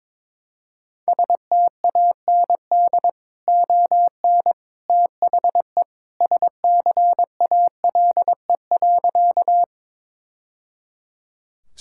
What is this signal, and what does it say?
Morse code, heard as a keyed tone, about 22 wpm, 700 Hz. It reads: STAND ON THE SCALE.
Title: Sentences from Top 1000 Words - Review Part 5 - 22wpm